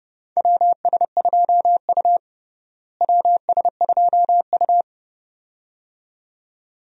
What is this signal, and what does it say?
Morse code, heard as a keyed tone, about 31 wpm, 700 Hz. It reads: WS2U WS2U
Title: USA Call Signs - 4 Characters - Repeated 2x - 30wpm